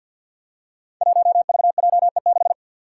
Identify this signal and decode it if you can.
Morse code, heard as a keyed tone, about 52 wpm, 700 Hz. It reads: 1VJE6
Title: Five Challenging Letter-Number Combinations 50wpm